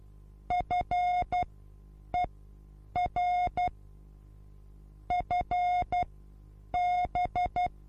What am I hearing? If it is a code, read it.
Morse code, heard as a keyed tone, about 12 wpm, spaced as for 8 wpm, 720 Hz. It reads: FER FB
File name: repos / Morsekurs / DJ4UF Morsen23